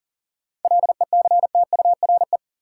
Morse code, heard as a keyed tone, about 40 wpm, 700 Hz. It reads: LECTURE
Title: Sentences from Top 1600 Words - New Word Drill - 40wpm